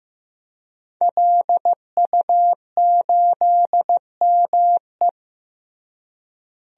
Morse code, heard as a keyed tone, about 15 wpm, 700 Hz. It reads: LU8ME
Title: DXCC-100 Call Signs - 5 Characters - Repeated 2x - 15wpm